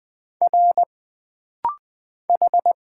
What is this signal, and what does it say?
Morse code, heard as a keyed tone, about 20 wpm, 700 Hz. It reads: R H
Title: Single Letters - Rapid-Fire 20wpm